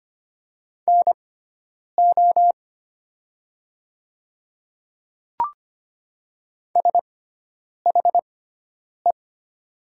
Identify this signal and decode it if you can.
Morse code, heard as a keyed tone, about 25 wpm, spaced as for 9 wpm, 700 Hz. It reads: NO SHE
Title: 6x Character Spacing - Top 100 Words - 25wpm